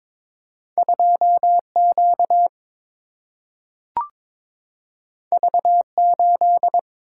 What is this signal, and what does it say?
Morse code, heard as a keyed tone, about 22 wpm, 700 Hz. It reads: 2 Q V 8
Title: Single-Letter Single-Number 22wpm